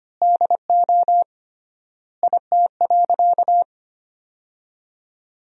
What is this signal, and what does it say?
Morse code, heard as a keyed tone, about 25 wpm, 700 Hz. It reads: DO IT.
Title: 3x Word-Spacing - Sentences from Top 100 Words - 25wpm